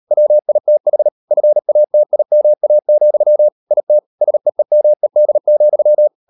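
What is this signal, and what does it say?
Morse code, heard as a keyed tone, about 38 wpm, 600 Hz. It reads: WITH FATIMA, IT SEEMED,
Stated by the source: Morse code practice at 38 WPM, sending Ken Follett's novel Never in English, from CW med Gnister.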